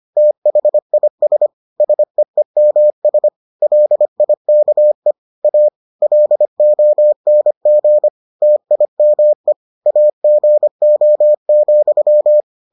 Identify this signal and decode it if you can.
Morse code, heard as a keyed tone, about 25 wpm, 600 Hz. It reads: THIS SEEMS LIKE A LONG TIME AGO,